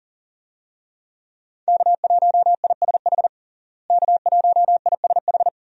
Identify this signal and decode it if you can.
Morse code, heard as a keyed tone, about 40 wpm, 700 Hz. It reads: K1ISH K1ISH
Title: USA Call Signs - 5 Characters - Repeated 3x - 40wpm